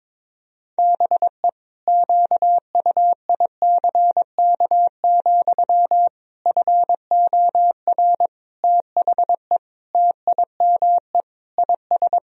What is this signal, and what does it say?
Morse code, heard as a keyed tone, about 22 wpm, 700 Hz. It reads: BE QUICK, FOR THE TIME IS
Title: Sentences from Top 1000 Words - Review Part 2 - 22wpm